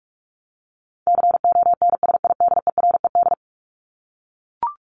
Morse code, heard as a keyed tone, about 44 wpm, 700 Hz. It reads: CONSIDERED
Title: ED Suffix - Top 100 Words with Base Words 45wpm